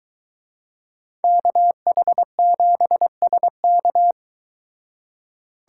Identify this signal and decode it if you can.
Morse code, heard as a keyed tone, about 23 wpm, 700 Hz. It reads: KH7SK